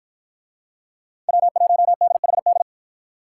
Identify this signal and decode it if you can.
Morse code, heard as a keyed tone, about 50 wpm, 700 Hz. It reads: W1DHD